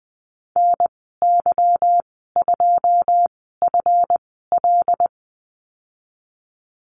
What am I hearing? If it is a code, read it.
Morse code, heard as a keyed tone, about 20 wpm, spaced as for 15 wpm, 700 Hz. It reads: NY2FL
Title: 173 Lesson 28g - Number 9 - Call Signs 20wpm